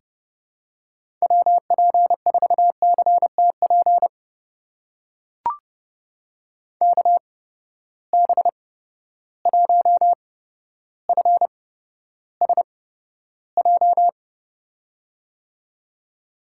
Morse code, heard as a keyed tone, about 30 wpm, spaced as for 13 wpm, 700 Hz. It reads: WP4CTP KB1FSJ
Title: USA Call Signs - 6 Characters - 8x Character Spacing - 30wpm